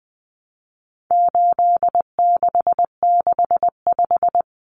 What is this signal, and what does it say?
Morse code, heard as a keyed tone, about 20 wpm, 700 Hz. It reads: 8665
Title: Common Cut Numbers - 4-Digit 20wpm